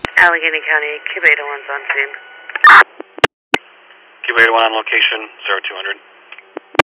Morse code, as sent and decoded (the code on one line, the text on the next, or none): none